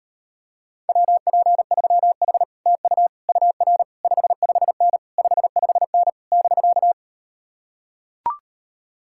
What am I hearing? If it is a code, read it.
Morse code, heard as a keyed tone, about 38 wpm, 700 Hz. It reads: WP3H TU UR 55N 55N <BK>